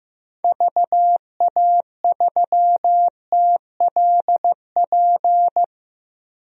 Morse code, heard as a keyed tone, about 15 wpm, 700 Hz. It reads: VA3TLP
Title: DXCC-100 Call Signs - 6 Characters - 15wpm